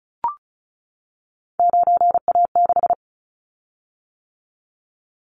Morse code, heard as a keyed tone, about 34 wpm, 700 Hz. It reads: 9A6
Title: Callsign Prefixes for Top 100 DXCC - 35wpm